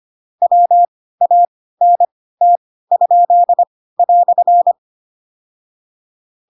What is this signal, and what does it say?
Morse code, heard as a keyed tone, about 25 wpm, spaced as for 16 wpm, 700 Hz. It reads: WANT?"
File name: Pinocchio / Pinocchio ch2 10 25 ik1hge